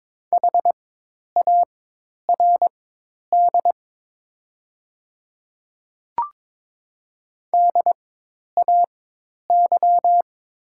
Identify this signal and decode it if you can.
Morse code, heard as a keyed tone, about 22 wpm, spaced as for 10 wpm, 700 Hz. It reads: HARD DAY
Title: 4x Character Spacing - Top 300 Words - 22wpm